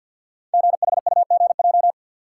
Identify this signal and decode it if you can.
Morse code, heard as a keyed tone, about 50 wpm, 700 Hz. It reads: GHUGJ
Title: Code Groups - 5 Letters - 50wpm